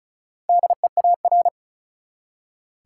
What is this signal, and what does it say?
Morse code, heard as a keyed tone, about 36 wpm, 700 Hz. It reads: DEAR
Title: Four Letter Words - 35wpm